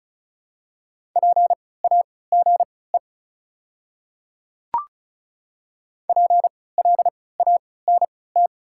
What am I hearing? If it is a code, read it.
Morse code, heard as a keyed tone, about 36 wpm, spaced as for 20 wpm, 690 Hz. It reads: PAGE PLANT
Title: 3x Character Spacing - Top 500 Words - 35wpm